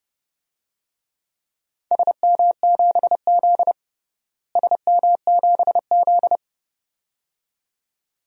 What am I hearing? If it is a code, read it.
Morse code, heard as a keyed tone, about 32 wpm, 700 Hz. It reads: SM7Z SM7Z
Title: DXCC-100 Call Signs - 4 Characters - Repeated 2x - 30wpm